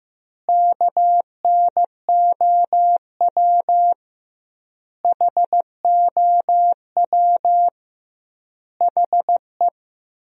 Morse code, heard as a keyed tone, about 15 wpm, 700 Hz. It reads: KNOW HOW HE
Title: 2x Word-Spacing - Sentences from Top 300 Words - 15wpm